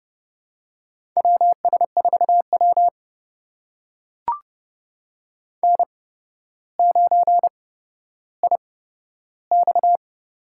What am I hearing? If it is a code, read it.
Morse code, heard as a keyed tone, about 30 wpm, spaced as for 13 wpm, 700 Hz. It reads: WS4W N9IX
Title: USA Call Signs - 4 Characters - 8x Character Spacing - 30wpm